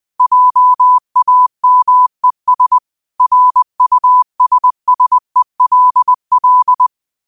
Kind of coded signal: Morse code